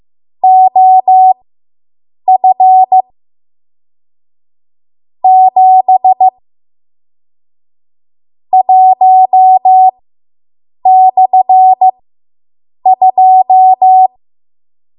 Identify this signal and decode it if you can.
Morse code, heard as a keyed tone, about 15 wpm, spaced as for 7 wpm, 750 Hz. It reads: OF 7 1/2